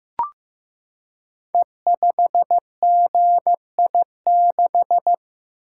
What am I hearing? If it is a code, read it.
Morse code, heard as a keyed tone, about 15 wpm, 700 Hz. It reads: E5GI6
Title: Code Groups - 5 Letter:Numbers - 15wpm